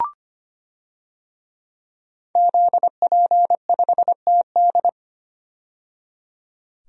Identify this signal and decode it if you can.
Morse code, heard as a keyed tone, about 25 wpm, 700 Hz. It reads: ZP5TD
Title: DXCC-100 Call Signs - 5 Characters - 25wpm